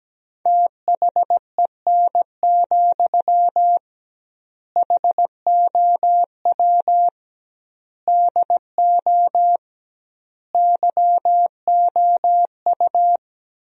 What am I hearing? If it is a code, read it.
Morse code, heard as a keyed tone, about 17 wpm, 700 Hz. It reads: THEN, HOW DO YOU
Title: 2x Word-Spacing - Sentences from Top 100 Words - 17wpm